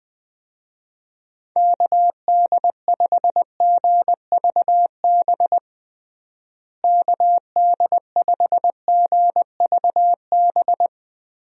Morse code, heard as a keyed tone, about 20 wpm, 700 Hz. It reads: KD5GVB KD5GVB
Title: USA Call Signs - 6 Characters - Repeated 2x - 20wpm